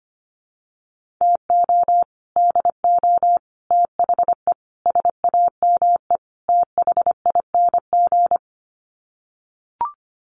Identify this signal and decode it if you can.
Morse code, heard as a keyed tone, about 25 wpm, 700 Hz. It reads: TO DO THE SAME THING
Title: Sets of 5 Words Encore 25wpm